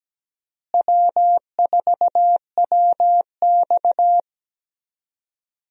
Morse code, heard as a keyed tone, about 17 wpm, 700 Hz. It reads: W4WX
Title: USA Call Signs - 4 Characters - Repeated 3x - 17wpm